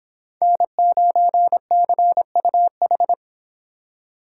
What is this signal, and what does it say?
Morse code, heard as a keyed tone, about 24 wpm, 700 Hz. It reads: N9CUH